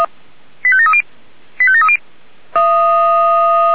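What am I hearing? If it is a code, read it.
Five-tone selective call (ZVEI), heard as quick runs of five tones, tones of about 70 ms. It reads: 85730 85730
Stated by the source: Received on an SDR.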